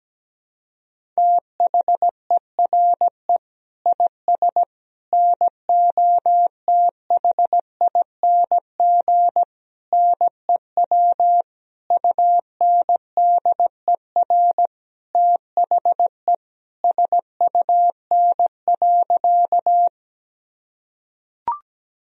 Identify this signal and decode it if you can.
Morse code, heard as a keyed tone, about 17 wpm, 700 Hz. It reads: THERE IS NOTHING NEW UNDER THE SUN.
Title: Sentences from Top 1000 Words - Review Part 1 - 17wpm